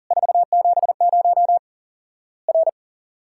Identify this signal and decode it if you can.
Morse code, heard as a keyed tone, about 40 wpm, 700 Hz. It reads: 470 R
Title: Worked All Europe DX Contest - QTC Exchange - 40wpm